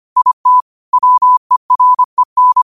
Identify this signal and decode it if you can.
Morse code, heard as a keyed tone, about 25 wpm, 1000 Hz. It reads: IT WEREN
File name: qotdfiles / 25wpm